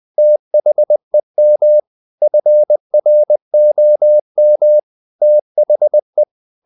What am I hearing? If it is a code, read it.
Morse code, heard as a keyed tone, about 20 wpm, 600 Hz. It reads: THEM FROM THE
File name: the old man and the sea / the old man and the sea part 1 20wpm